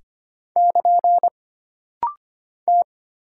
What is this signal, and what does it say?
Morse code, heard as a keyed tone, about 25 wpm, 700 Hz. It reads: <KN> T
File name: Single Character (Letter:Number:Punctuation:Prosign) - Rapid-Fire - 25wpm